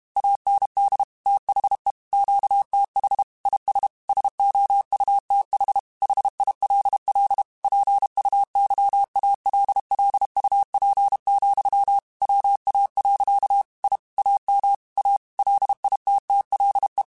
Morse code, heard as a keyed tone, about 32 wpm, 800 Hz. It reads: AND THE QTH IS SOUTH HILL PUYALLUP, WA. I AM A LITTLE